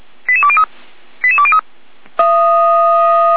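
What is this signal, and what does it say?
Five-tone selective call (ZVEI), heard as quick runs of five tones, tones of about 70 ms. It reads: 88203 88203
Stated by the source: ZVEI five-tone selective call received on an SDR, recorded by the Feuerwehr Orlamuende fire brigade.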